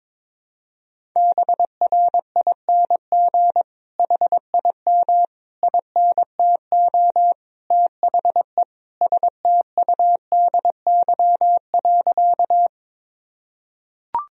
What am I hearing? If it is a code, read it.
Morse code, heard as a keyed tone, about 22 wpm, 700 Hz. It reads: BRING HIM INTO THE STUDY.
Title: Sentences from Top 1000 Words - Review Part 1 - 22wpm